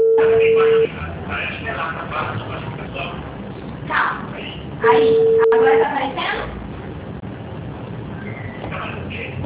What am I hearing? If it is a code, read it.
Ringback tone (China, 450 Hz), with speech.
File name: out- -2018- - - .68341